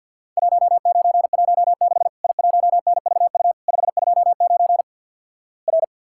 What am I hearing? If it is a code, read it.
Morse code, heard as a keyed tone, about 50 wpm, 700 Hz. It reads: 1916 I1NVU 529 R
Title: Worked All Europe DX Contest - QTC Exchange - 50wpm